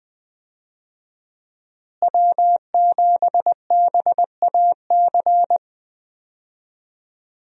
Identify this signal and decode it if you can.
Morse code, heard as a keyed tone, about 20 wpm, 700 Hz. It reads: W7BAC
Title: USA Call Signs - 5 Characters - Repeated 3x - 20wpm